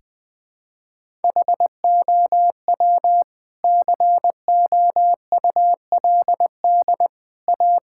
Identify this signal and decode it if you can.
Morse code, heard as a keyed tone, about 20 wpm, 700 Hz. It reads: HOW COULD A
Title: Sentences from Top 1000 Words - Review Part 6 - 20wpm